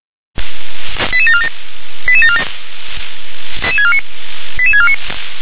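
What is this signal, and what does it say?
Five-tone selective call (ZVEI), heard as quick runs of five tones, tones of about 70 ms. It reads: 88747 88747 88649
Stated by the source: ZVEI five-tone selective call received with an SDR.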